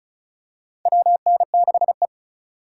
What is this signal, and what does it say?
Morse code, heard as a keyed tone, about 36 wpm, 700 Hz. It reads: WN6E
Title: USA Call Signs - 4 Characters - 3x Character Spacing - 35wpm